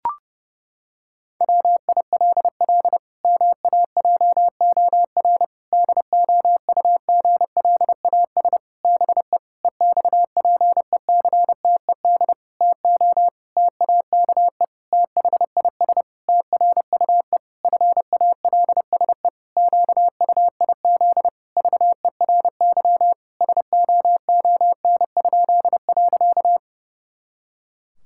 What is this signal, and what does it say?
Morse code, heard as a keyed tone, about 30 wpm, 700 Hz. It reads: WILL MAJOR DOUGLAS BE EXPECTED TO TAKE THIS TRUE FALSE QUIZ VERY SOON?.
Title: Pangram Challenge v2 - 30wpm